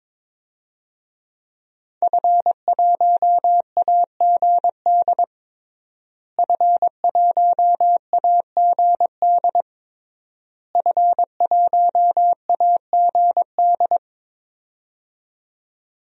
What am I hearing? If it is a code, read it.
Morse code, heard as a keyed tone, about 22 wpm, 700 Hz. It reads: F1AGD F1AGD F1AGD
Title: DXCC-100 Call Signs - 5 Characters - Repeated 3x - 22wpm